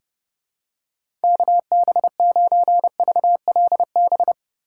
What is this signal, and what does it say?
Morse code, heard as a keyed tone, about 30 wpm, 700 Hz. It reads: KB9VLB
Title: USA Call Signs - 6 Characters - 2x Character Spacing - 30wpm